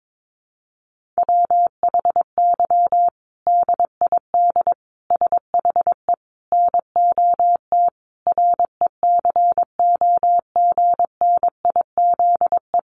Morse code, heard as a keyed tone, about 22 wpm, 700 Hz. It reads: WHY DID SHE NOT RECOGNIZE
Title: Sentences from Top 1400 Words 22wpm